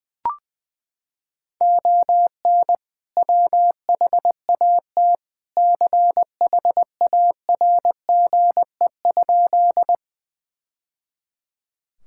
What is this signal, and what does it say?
Morse code, heard as a keyed tone, about 21 wpm, 700 Hz. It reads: ON WHAT CHARGE?
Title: Sentences from Top 1000 Words - Review Part 6 - 20wpm